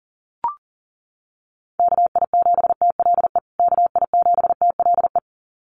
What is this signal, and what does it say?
Morse code, heard as a keyed tone, about 40 wpm, 700 Hz. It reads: KI7TLE KI7TLE
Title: USA Call Signs Repeated 2x 40wpm